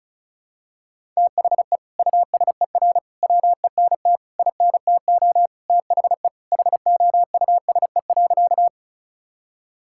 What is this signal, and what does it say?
Morse code, heard as a keyed tone, about 36 wpm, 700 Hz. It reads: THE USER WENT INTO THE HOUSE.
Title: Sentences from Top 1000 Words - Review Part 5 - 35wpm